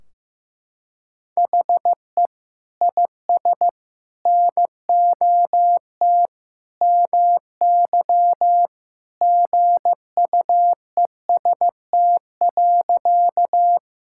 Morse code, heard as a keyed tone, about 16 wpm, 700 Hz. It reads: HE IS NOT MY GUEST.
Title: Sentences from Top 1600 Words 15wpm